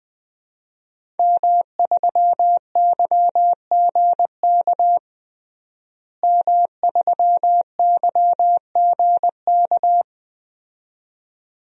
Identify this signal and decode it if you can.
Morse code, heard as a keyed tone, about 20 wpm, 700 Hz. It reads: M3YGK M3YGK
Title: DXCC-100 Call Signs - 5 Characters - Repeated 2x - 20wpm